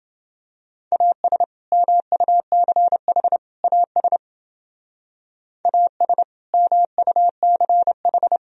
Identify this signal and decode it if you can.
Morse code, heard as a keyed tone, about 30 wpm, 700 Hz. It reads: AS MUCH AS AS MUCH